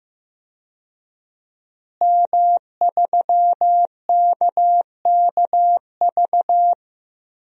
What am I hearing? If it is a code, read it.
Morse code, heard as a keyed tone, about 15 wpm, 700 Hz. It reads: M3KKV
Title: DXCC-100 Call Signs - 5 Characters - Repeated 3x - 15wpm